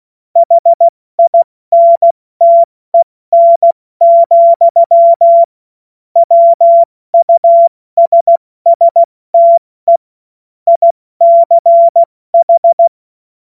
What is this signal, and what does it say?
Morse code, heard as a keyed tone, about 16 wpm, 680 Hz. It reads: HINTEN, WUSSTE ICH
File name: Morsekurs / Witze / Witz 28. Dezember 2021, 70 F80